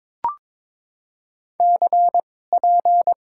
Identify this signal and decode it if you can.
Morse code, heard as a keyed tone, about 22 wpm, spaced as for 16 wpm, 700 Hz. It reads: CP
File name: Top 90 QSO Elements - 2x Character Spacing - 22wpm